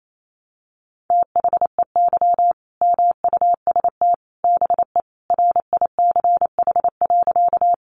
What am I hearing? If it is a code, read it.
Morse code, heard as a keyed tone, about 28 wpm, 700 Hz. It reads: THEY MUST BE RICH.
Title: Sentences from Top 1100 Words 28wpm